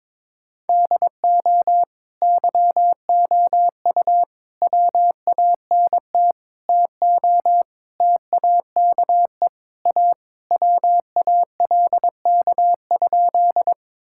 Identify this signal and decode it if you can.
Morse code, heard as a keyed tone, about 22 wpm, 700 Hz. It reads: DO YOU WANT TO TAKE A WALK?
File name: practice / 22wpm / Sentences with 3-Word Phrases - 22wpm